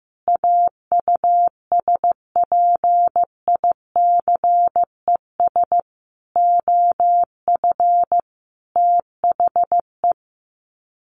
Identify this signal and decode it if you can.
Morse code, heard as a keyed tone, about 15 wpm, 700 Hz. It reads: AUSPICES OF THE
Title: Sets of 5 Words Encore2 15wpm